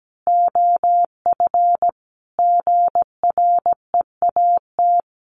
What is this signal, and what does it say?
Morse code, heard as a keyed tone, about 17 wpm, 700 Hz. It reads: OF GREAT
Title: Sentences from Top 1200 Words 17wpm